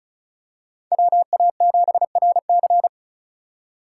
Morse code, heard as a keyed tone, about 34 wpm, 700 Hz. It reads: WA7RC
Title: USA Call Signs - 5 Characters - 2x Character Spacing - 35wpm